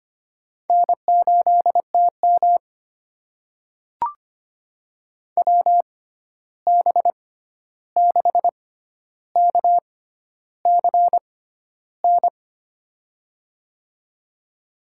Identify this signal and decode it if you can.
Morse code, heard as a keyed tone, about 25 wpm, spaced as for 11 wpm, 700 Hz. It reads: N8TM WB6KCN
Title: USA Call Signs - 6x Character Spacing - 25wpm